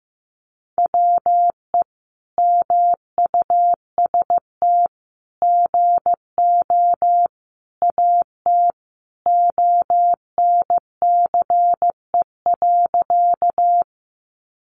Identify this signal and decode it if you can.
Morse code, heard as a keyed tone, about 15 wpm, 700 Hz. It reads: WE MUST GO AT ONCE.
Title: Sentences from Top 300 Words 15wpm